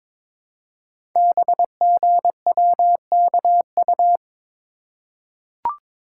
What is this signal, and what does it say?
Morse code, heard as a keyed tone, about 22 wpm, 690 Hz. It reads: BGWKU
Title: Code Groups - 5 Letters - 22wpm